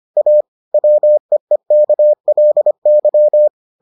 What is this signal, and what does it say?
Morse code, heard as a keyed tone, about 25 wpm, 600 Hz. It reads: A WEEKLY